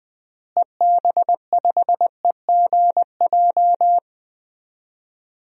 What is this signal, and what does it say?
Morse code, heard as a keyed tone, about 20 wpm, 700 Hz. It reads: EB5EGJ
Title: DXCC-100 Call Signs - 6 Characters - Repeated 2x - 20wpm